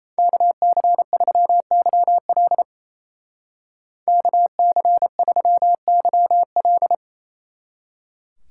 Morse code, heard as a keyed tone, about 35 wpm, 700 Hz. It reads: KC3YL KC3YL
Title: Speed Racing - USA Call Signs 28wpm